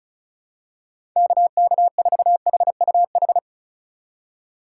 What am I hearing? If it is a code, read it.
Morse code, heard as a keyed tone, about 34 wpm, 700 Hz. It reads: KK4HUH
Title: USA Call Signs - 6 Characters - 3x Character Spacing - 35wpm